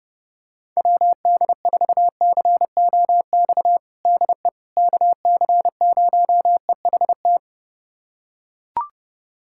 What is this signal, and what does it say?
Morse code, heard as a keyed tone, about 30 wpm, 700 Hz. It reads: WD4COX DE KC0EHT
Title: USA Callsign DE Callsign 30wpm